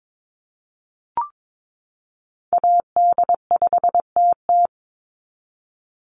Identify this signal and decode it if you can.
Morse code, heard as a keyed tone, about 22 wpm, 700 Hz. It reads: AD5TT
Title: Call Signs Repeated 3x - 22wpm